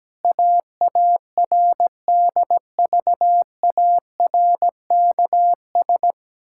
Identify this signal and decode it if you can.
Morse code, heard as a keyed tone, about 17 wpm, 700 Hz. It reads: AARDVARKS
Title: Pangram Challenge v2 - 17wpm